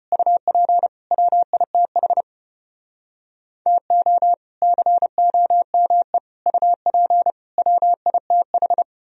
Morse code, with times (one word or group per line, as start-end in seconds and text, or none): 0.1-0.9 UP
1.1-2.2 WITH
3.7-4.3 TO
4.6-6.2 COME
6.5-7.3 UP
7.6-8.8 WITH